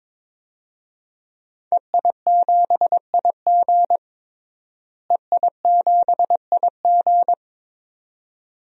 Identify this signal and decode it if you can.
Morse code, heard as a keyed tone, about 22 wpm, 700 Hz. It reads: EI7IG EI7IG
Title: DXCC-100 Call Signs - 5 Characters - Repeated 2x - 22wpm